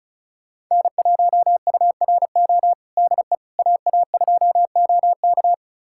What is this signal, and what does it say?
Morse code, heard as a keyed tone, about 36 wpm, 700 Hz. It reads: N1URO DE AA2OK
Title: USA Callsign DE Callsign - 5 Characters - 35wpm